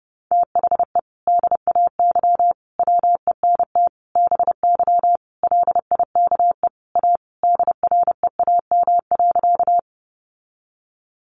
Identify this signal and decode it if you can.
Morse code, heard as a keyed tone, about 30 wpm, 700 Hz. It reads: THE DAY WENT BY LIKE A DREAM.
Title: Sentences from Top 1500 Words 30wpm